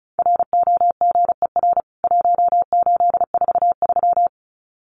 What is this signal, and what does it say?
Morse code, heard as a keyed tone, about 34 wpm, 700 Hz. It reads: ROGER 1843
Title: CWT Exchange 35wpm